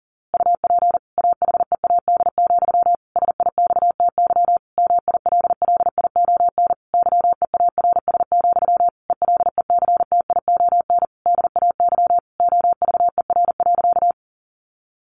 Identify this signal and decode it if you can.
Morse code, heard as a keyed tone, about 38 wpm, 700 Hz. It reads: UP AHEAD, SIXTY MILLION YEARS, ELECTION DAY OVER.
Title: A Sound Of Thunder 40wpm